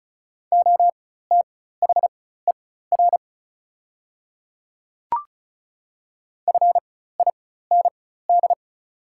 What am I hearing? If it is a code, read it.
Morse code, heard as a keyed tone, about 35 wpm, spaced as for 16 wpm, 700 Hz. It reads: OTHER FIND
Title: 4x Character Spacing - Top 100 Words - 35wpm